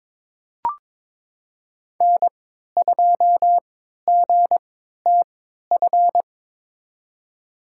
Morse code, heard as a keyed tone, about 22 wpm, spaced as for 13 wpm, 700 Hz. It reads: N2GTF
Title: USA Call Signs - 5 Characters - 3x Character Spacing - 22wpm